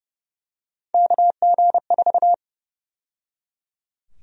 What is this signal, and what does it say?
Morse code, heard as a keyed tone, about 30 wpm, 700 Hz. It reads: KG4
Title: USA Call Sign Prefixes - 30wpm